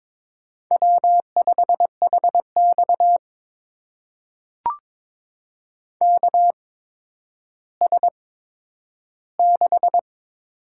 Morse code, heard as a keyed tone, about 22 wpm, spaced as for 11 wpm, 700 Hz. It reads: W5HX KS6
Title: USA Call Signs - 8x Character Spacing - 22wpm